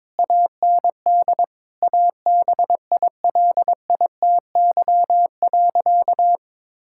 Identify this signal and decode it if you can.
Morse code, heard as a keyed tone, about 22 wpm, 700 Hz. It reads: AND ABILITY.
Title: Sentences from Top 1000 Words - Review Part 6 - 22wpm